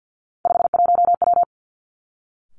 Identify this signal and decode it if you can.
Morse code, heard as a keyed tone, about 50 wpm, 700 Hz. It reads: 51W
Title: Three Challenging Letter-Number Combinations 50wpm